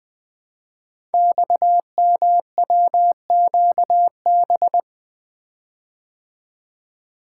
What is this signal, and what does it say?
Morse code, heard as a keyed tone, about 21 wpm, 700 Hz. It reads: XMWQB
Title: Code Groups - 5 Letter:Numbers - 20wpm